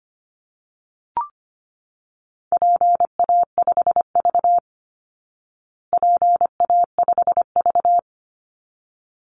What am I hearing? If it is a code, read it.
Morse code, heard as a keyed tone, about 25 wpm, 700 Hz. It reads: PA5V PA5V
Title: Call Signs Repeated 3x - 25wpm